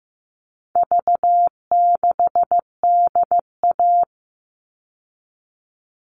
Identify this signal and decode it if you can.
Morse code, heard as a keyed tone, about 15 wpm, 700 Hz. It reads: V6DA
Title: All Cut Numbers - 4-Digit 15wpm